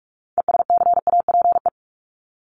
Morse code, heard as a keyed tone, about 44 wpm, 700 Hz. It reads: ESCAPE
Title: Top 1501-1750 Words 45wpm